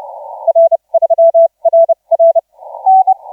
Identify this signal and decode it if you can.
Morse code, heard as a keyed tone, about 31 wpm, spaced as for 25 wpm, 660 Hz, with radio noise. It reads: R3RR
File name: 7S6W 0740UTC 17m